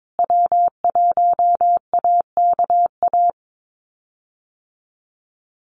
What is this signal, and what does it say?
Morse code, heard as a keyed tone, about 22 wpm, 700 Hz. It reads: W1AKA
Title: USA Call Signs Repeated 3x 22wpm